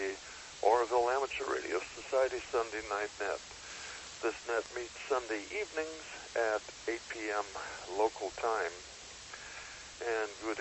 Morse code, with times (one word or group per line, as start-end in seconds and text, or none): none